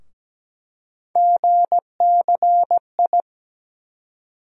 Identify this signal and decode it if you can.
Morse code, heard as a keyed tone, about 17 wpm, 700 Hz. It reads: GCI
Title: USA Call Sign Postfixes - 3 Letters 17wpm